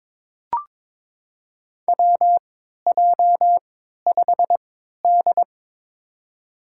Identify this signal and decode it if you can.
Morse code, heard as a keyed tone, about 22 wpm, spaced as for 12 wpm, 700 Hz. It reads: WJ5D
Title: USA Call Signs - 4 Characters - 3x Character Spacing - 22wpm